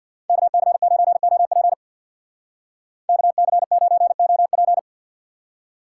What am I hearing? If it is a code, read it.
Morse code, heard as a keyed tone, about 59 wpm, 700 Hz. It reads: KC9OP KC9OP
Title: Speed Racing - USA 2x2 Call Signs 50wpm